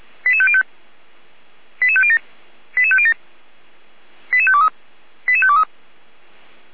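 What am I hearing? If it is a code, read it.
Five-tone selective call (ZVEI), heard as quick runs of five tones, tones of about 80 ms. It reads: 88505 88507 88507 88523 88523
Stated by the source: ZVEI five-tone call received on an SDR.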